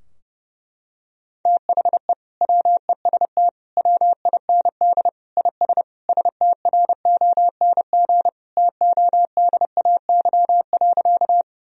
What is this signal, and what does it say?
Morse code, heard as a keyed tone, about 30 wpm, 700 Hz. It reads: THE WEST WIND IS STRONG TODAY.
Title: Sentences from Top 1000 Words - Review Part 2 - 30wpm